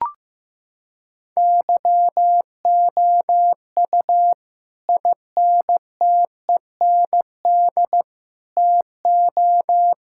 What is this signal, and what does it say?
Morse code, heard as a keyed tone, about 16 wpm, 700 Hz. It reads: YOU INTEND TO
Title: Sentences from Top 1600 Words 15wpm